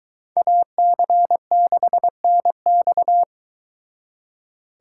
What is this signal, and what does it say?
Morse code, heard as a keyed tone, about 23 wpm, 700 Hz. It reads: AC6NX